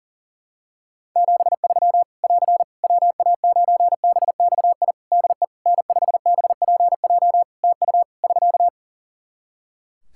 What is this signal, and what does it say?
Morse code, heard as a keyed tone, about 40 wpm, 700 Hz. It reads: 73 <AR> WA9BXI DE N5BPJ TU <SK>
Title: K1USN SST QSO Practice 40wpm